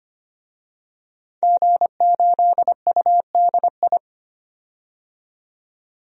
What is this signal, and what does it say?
Morse code, heard as a keyed tone, about 25 wpm, 700 Hz. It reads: G8UDI
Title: DXCC-100 Call Signs - 5 Characters - 25wpm